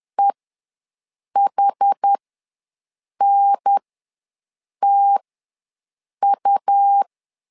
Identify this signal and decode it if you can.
Morse code, heard as a keyed tone, about 10 wpm, spaced as for 6 wpm, 800 Hz. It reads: EHNTU
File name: 2 UHDN ws 10wpm